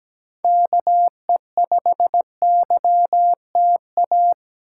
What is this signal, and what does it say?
Morse code, heard as a keyed tone, about 22 wpm, 700 Hz. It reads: KE5YTA